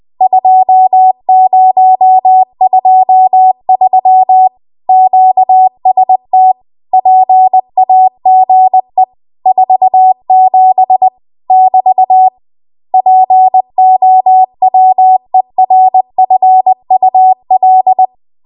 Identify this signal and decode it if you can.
Morse code, heard as a keyed tone, about 20 wpm, 750 Hz. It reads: 2023 QST PAGE 47 <BT> POWERFUL